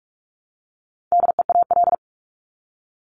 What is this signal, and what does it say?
Morse code, heard as a keyed tone, about 44 wpm, 700 Hz. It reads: DEAL
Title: ING Suffix - Top 100 Words with Base Words 45wpm